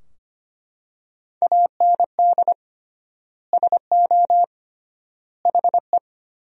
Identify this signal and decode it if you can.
Morse code, heard as a keyed tone, about 25 wpm, 700 Hz. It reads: AND SO HE
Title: 3x Word-Spacing - Sentences from Top 500 Words - 25wpm